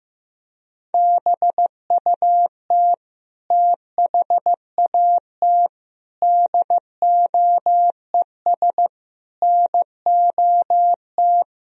Morse code, heard as a keyed tone, about 15 wpm, 700 Hz. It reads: BUT THAT DOES NOT
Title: Speed-Racing - Sets of 5 Words 15wpm